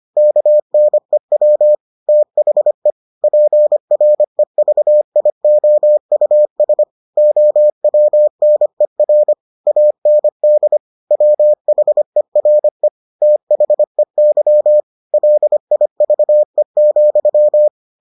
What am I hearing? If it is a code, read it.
Morse code, heard as a keyed tone, about 25 wpm, 600 Hz. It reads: KNEW THE PREVIOUS OWNER AND WHERE THEY LIVE,